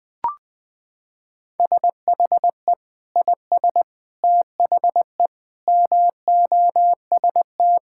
Morse code, heard as a keyed tone, about 20 wpm, 700 Hz. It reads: SHE IS THE MOST
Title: Sentences with 2-Word Phrases - 20wpm